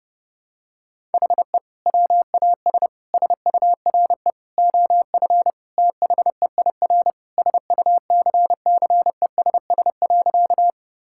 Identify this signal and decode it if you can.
Morse code, heard as a keyed tone, about 31 wpm, 700 Hz. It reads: HE WAS SURE OF THEIR SUCCESS.
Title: Sentences from Top 1000 Words - Review Part 6 - 30wpm